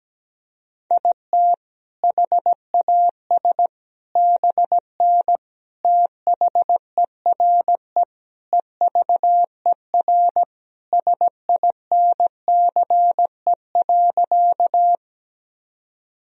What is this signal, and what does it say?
Morse code, heard as a keyed tone, about 17 wpm, 700 Hz. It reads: IT HAS BN THERE EVER SINCE.
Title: Sentences from Top 400 Words and Abbreviations 17wpm